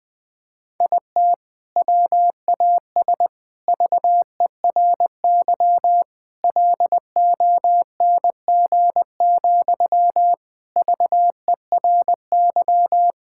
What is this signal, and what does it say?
Morse code, heard as a keyed tone, about 21 wpm, 700 Hz. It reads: IT WAS VERY LONG, VERY
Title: Sentences from Top 1600 Words 20wpm